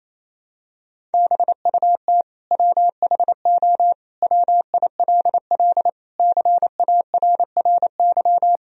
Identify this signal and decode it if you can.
Morse code, heard as a keyed tone, about 28 wpm, 700 Hz. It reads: BUT WHO WILL CARRY
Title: Sentences from Top 1000 Words - Review Part 1 - 28wpm